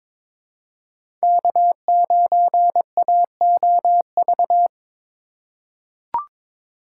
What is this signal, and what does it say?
Morse code, heard as a keyed tone, about 22 wpm, 700 Hz. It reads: K9AOV
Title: USA Call Signs - 5 Characters - 2x Character Spacing - 22wpm